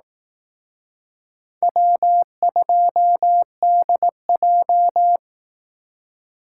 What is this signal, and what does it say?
Morse code, heard as a keyed tone, about 18 wpm, 700 Hz. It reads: W2DJ
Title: Speed Racing - USA 1x2 Call Signs 15wpm